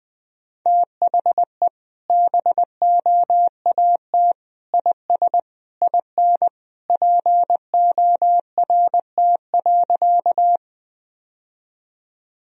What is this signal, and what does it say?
Morse code, heard as a keyed tone, about 20 wpm, 700 Hz. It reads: THE BOAT IS IN PORT.
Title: Sentences from Top 1000 Words - Review Part 1 - 20wpm